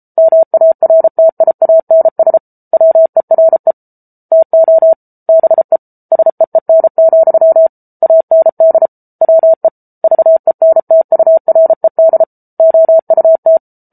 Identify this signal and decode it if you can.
Morse code, heard as a keyed tone, about 34 wpm, 650 Hz. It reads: MARTIANS WERE TO BE SEEN, AND WE VENTURED OUT